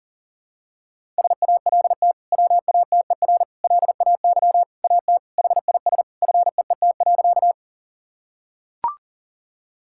Morse code, heard as a keyed tone, about 40 wpm, 700 Hz. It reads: SALT WATER LAY AT HIS FEET.
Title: Sentences from Top 1000 Words - Review Part 5 - 40wpm